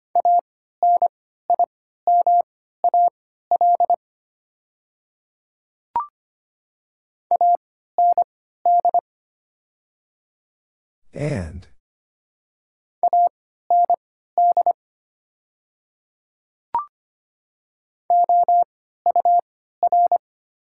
Morse code, heard as a keyed tone, about 25 wpm, spaced as for 14 wpm, 700 Hz. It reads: ANIMAL AND AND OUR
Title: 3x Character Spacing - Top 200 Words - 25wpm